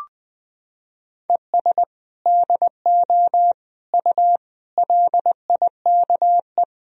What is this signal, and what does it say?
Morse code, heard as a keyed tone, about 20 wpm, 700 Hz. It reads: ES DO U LIKE
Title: Sentences from Top 100 Words and Abbreviations 20wpm